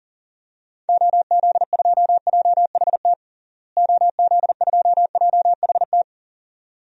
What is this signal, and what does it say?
Morse code, heard as a keyed tone, about 40 wpm, 700 Hz. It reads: OZ2JHT OZ2JHT
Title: DXCC-100 Call Signs - 6 Characters - Repeated 2x - 40wpm